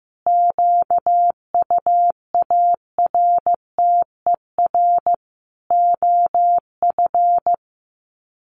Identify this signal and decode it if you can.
Morse code, heard as a keyed tone, about 15 wpm, 700 Hz. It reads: QUARTER OF
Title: Sets of 5 Words Encore2 15wpm